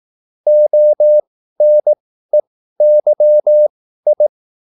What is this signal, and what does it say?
Morse code, heard as a keyed tone, about 18 wpm, spaced as for 13 wpm, 600 Hz. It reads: ONEYI